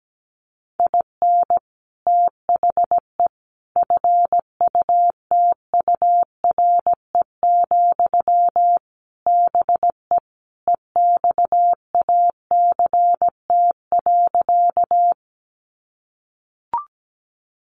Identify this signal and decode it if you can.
Morse code, heard as a keyed tone, about 17 wpm, 700 Hz. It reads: IN THE FUTURE, BE EXACT.
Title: Sentences from Top 1300 Words 17wpm